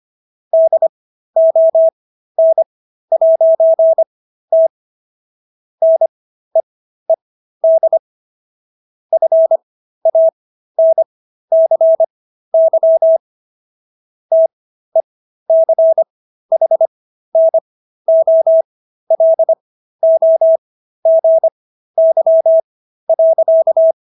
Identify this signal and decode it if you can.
Morse code, heard as a keyed tone, about 25 wpm, spaced as for 13 wpm, 650 Hz. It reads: DON'T NEED FANCY TECHNOLOGY.